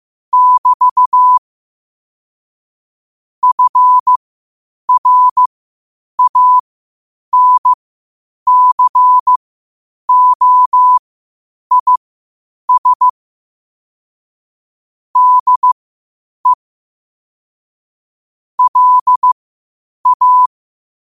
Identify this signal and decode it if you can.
Morse code, heard as a keyed tone, about 15 wpm, spaced as for 8 wpm, 1000 Hz. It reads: <BT> FRANCOIS DE LA